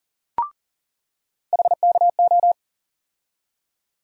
Morse code, heard as a keyed tone, about 40 wpm, 700 Hz. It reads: HKO
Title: USA Call Sign Postfixes - 3 Letters 40wpm